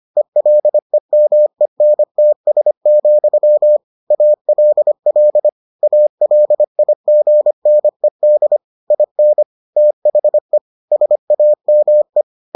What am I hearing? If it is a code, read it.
Morse code, heard as a keyed tone, about 25 wpm, 600 Hz. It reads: ELEMENTS, ALL ALIGNED IN THE SAME